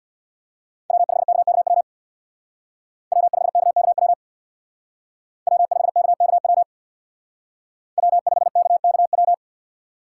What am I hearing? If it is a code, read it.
Morse code, heard as a keyed tone, about 62 wpm, 700 Hz. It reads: W5KKW W5KKW W5KKW W5KKW
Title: Speed Racing - USA 1x3 Call Signs 50wpm